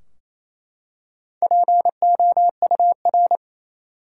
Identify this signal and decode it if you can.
Morse code, heard as a keyed tone, about 27 wpm, 700 Hz. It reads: POUR
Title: Four Letter Words - 28wpm